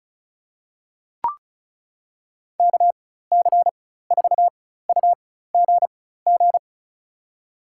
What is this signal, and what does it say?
Morse code, heard as a keyed tone, about 34 wpm, spaced as for 16 wpm, 690 Hz. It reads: KC4UGG
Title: USA Call Signs - 6 Characters - 4x Character Spacing - 35wpm